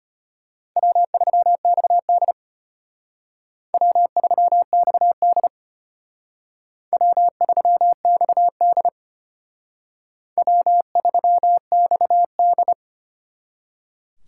Morse code, heard as a keyed tone, about 32 wpm, 700 Hz. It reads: W3XD W3XD W3XD W3XD